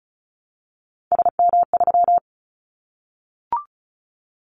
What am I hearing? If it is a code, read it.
Morse code, heard as a keyed tone, about 34 wpm, 700 Hz. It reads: SM3
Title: Callsign Prefixes for Top 25 DXCC - Repeated 3x - 35wpm